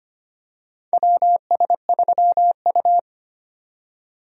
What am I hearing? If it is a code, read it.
Morse code, heard as a keyed tone, about 25 wpm, 700 Hz. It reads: WS3U
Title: USA Call Signs - 4 Characters - 25wpm